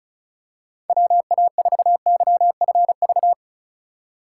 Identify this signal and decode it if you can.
Morse code, heard as a keyed tone, about 36 wpm, 700 Hz. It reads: WA4YFV
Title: USA Call Signs - 8x Character Spacing - 35wpm